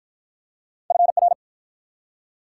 Morse code, heard as a keyed tone, about 54 wpm, 700 Hz. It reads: FR